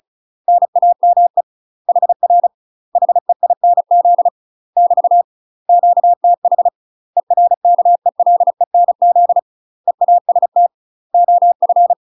Morse code, heard as a keyed tone, about 36 wpm, 700 Hz. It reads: NAME HR HEINZ <BT> QTH ERKELENZ EAST OF
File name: UK MASTER QSOs TRAINING/35WPM